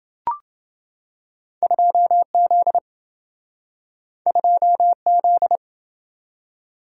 Morse code, heard as a keyed tone, about 27 wpm, 700 Hz. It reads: 2Z 2Z